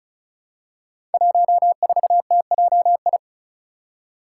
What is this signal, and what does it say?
Morse code, heard as a keyed tone, about 36 wpm, 700 Hz. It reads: 14TJI
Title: Five Challenging Letter-Number Combinations 35wpm